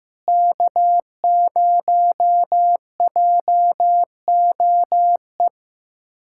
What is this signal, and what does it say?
Morse code, heard as a keyed tone, about 15 wpm, 700 Hz. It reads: K0JOE
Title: USA Callsign DE Callsign 15wpm